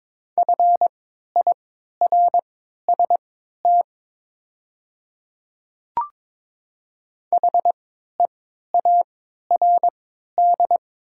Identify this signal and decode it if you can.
Morse code, heard as a keyed tone, about 22 wpm, spaced as for 12 wpm, 700 Hz. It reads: FIRST HEARD
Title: 3x Character Spacing - Top 500 Words - 22wpm